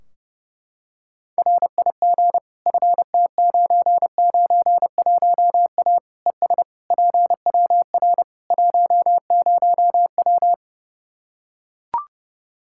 Morse code, heard as a keyed tone, about 30 wpm, 700 Hz. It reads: RIG FT991A ES PWR 10W
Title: Practice QSO - Exchange 2 - Info - 30wpm